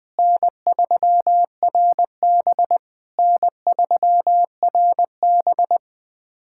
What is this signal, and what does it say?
Morse code, heard as a keyed tone, about 20 wpm, 700 Hz. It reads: N3RB N3RB
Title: SOTA QSO Practice 20wpm